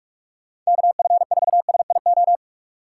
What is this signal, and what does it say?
Morse code, heard as a keyed tone, about 47 wpm, 700 Hz. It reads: KF4SIO
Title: USA Call Signs - 3x Character Spacing - 45wpm